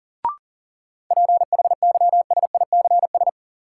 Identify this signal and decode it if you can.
Morse code, heard as a keyed tone, about 37 wpm, 700 Hz. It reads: PHYSICS
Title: Sentences from Top 1500 Words - New Word Drill Alt - 40wpm